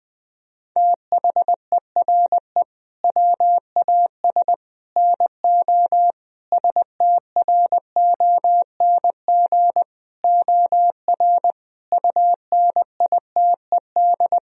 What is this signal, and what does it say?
Morse code, heard as a keyed tone, about 20 wpm, 700 Hz. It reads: THERE WAS NO STRONG OR UNITED
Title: Sentences from Top 1000 Words - Review Part 4 - 20wpm